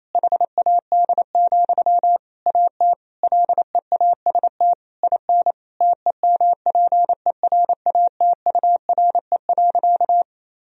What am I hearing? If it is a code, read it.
Morse code, heard as a keyed tone, about 28 wpm, 700 Hz. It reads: HAD, AT LEAST IN TEMPERATURE.
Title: Sentences from Top 1000 Words - Review Part 4 - 28wpm